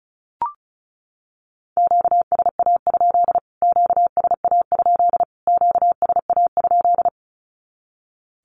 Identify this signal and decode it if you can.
Morse code, heard as a keyed tone, about 34 wpm, 700 Hz. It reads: QSA? QSA? QSA?
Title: Q-Codes Repeated 3x 35wpm